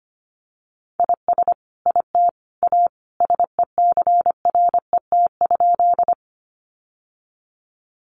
Morse code, heard as a keyed tone, about 25 wpm, 700 Hz. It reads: IS IT A SECRET?
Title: Sentences from Top 1200 Words 25wpm